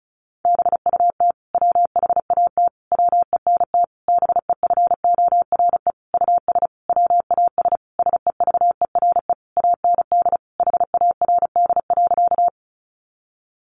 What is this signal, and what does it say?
Morse code, heard as a keyed tone, about 34 wpm, 700 Hz. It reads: BUT WHAT WENT BEFORE US WAS SEVERE AND HARD.
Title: Sentences from Top 1400 Words 35wpm